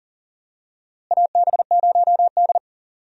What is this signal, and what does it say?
Morse code, heard as a keyed tone, about 40 wpm, 700 Hz. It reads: AB0D
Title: USA Call Signs - 4 Characters - Repeated 2x - 40wpm